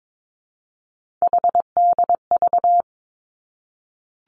Morse code, heard as a keyed tone, about 22 wpm, 700 Hz. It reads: HDV
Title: Three Challenging Letter-Number Combinations 22wpm